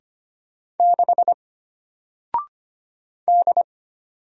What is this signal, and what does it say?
Morse code, heard as a keyed tone, about 25 wpm, 700 Hz. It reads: 6 D
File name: Single-Letter-Number-25wpm